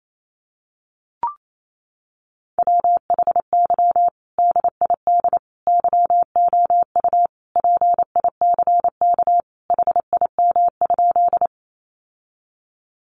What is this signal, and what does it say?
Morse code, heard as a keyed tone, about 28 wpm, 700 Hz. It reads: WHY DID YOU PICK HIM?
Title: Sentences from Top 1000 Words 28wpm